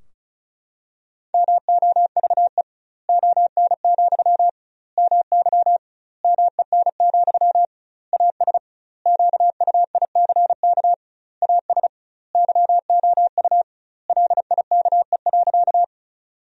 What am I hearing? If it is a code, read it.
Morse code, heard as a keyed tone, about 36 wpm, 700 Hz. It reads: MOVE ON, MY MEN, AS QUICK AS YOU LIKE.
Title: 2x Word-Spacing - Sentences from Top 500 Words - 35wpm